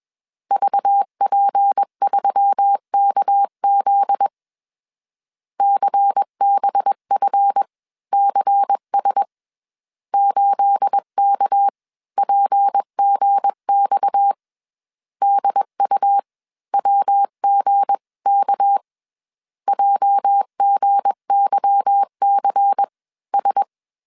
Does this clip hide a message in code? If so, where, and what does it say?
Yes: Morse code from 0.5 to 23.6 s, heard as a keyed tone, about 21 wpm, spaced as for 18 wpm, 800 Hz. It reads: VP3KZ CBFCS 8KPGX DUWGK JGYCS